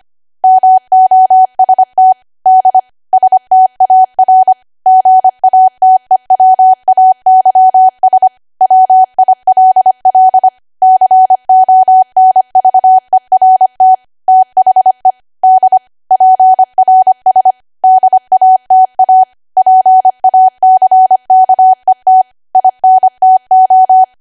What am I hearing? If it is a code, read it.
Morse code, heard as a keyed tone, about 25 wpm, 750 Hz. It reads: MOST D STAR GATEWAYS WILL CONVERT THE D PRS DATA PACKET INTO
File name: Archive / 25 WPM / 25WPM